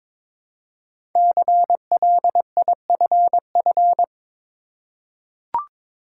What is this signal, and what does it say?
Morse code, heard as a keyed tone, about 22 wpm, 690 Hz. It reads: CLIFF
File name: 5 Letter Words - First 3 Letters Given - 22wpm